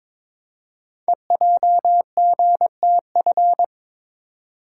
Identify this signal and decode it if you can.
Morse code, heard as a keyed tone, about 22 wpm, 700 Hz. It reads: EJGTF
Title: Code Groups - 5 Letters - 22wpm